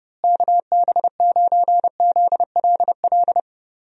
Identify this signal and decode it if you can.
Morse code, heard as a keyed tone, about 31 wpm, 700 Hz. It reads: KB9ZLL
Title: USA Call Signs - 6 Characters - Repeated 3x - 30wpm